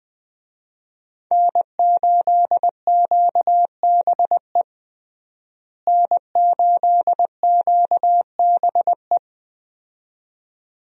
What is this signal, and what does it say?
Morse code, heard as a keyed tone, about 21 wpm, 700 Hz. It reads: N8QBE N8QBE
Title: USA Call Signs - 5 Characters - Repeated 2x - 20wpm